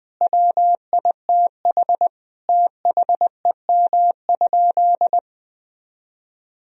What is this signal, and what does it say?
Morse code, heard as a keyed tone, about 20 wpm, 700 Hz. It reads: WITH THEM?
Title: Sentences from Top 1000 Words - Review Part 2 - 20wpm